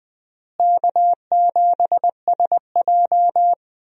Morse code, heard as a keyed tone, about 20 wpm, 700 Hz. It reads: K7SJ
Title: USA Call Signs - 4 Characters - Repeated 2x - 20wpm